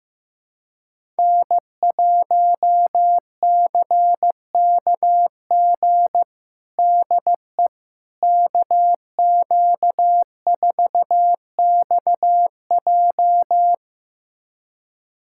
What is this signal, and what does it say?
Morse code, heard as a keyed tone, about 15 wpm, 700 Hz. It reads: N1CKG DE KQ4XJ
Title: USA Callsign DE Callsign - 5 Characters - 15wpm